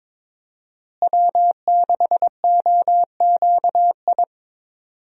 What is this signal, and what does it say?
Morse code, heard as a keyed tone, about 22 wpm, 700 Hz. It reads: W6OQI
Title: USA Call Signs - 5 Characters - Repeated 3x - 22wpm